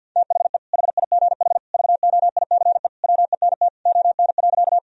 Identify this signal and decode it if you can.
Morse code, heard as a keyed tone, about 50 wpm, 700 Hz. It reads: THE HIGH VOICE WENT ON.
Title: Sentences from Top 1000 Words - Review Part 1 - 50wpm